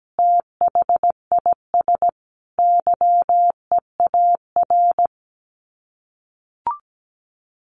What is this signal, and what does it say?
Morse code, heard as a keyed tone, about 17 wpm, 700 Hz. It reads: THIS YEAR
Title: Sets of 5 Words Encore 17wpm